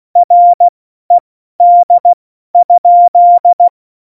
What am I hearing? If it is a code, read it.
Morse code, heard as a keyed tone, about 16 wpm, spaced as for 12 wpm, 700 Hz. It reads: RED?
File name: Morsekurs / Witze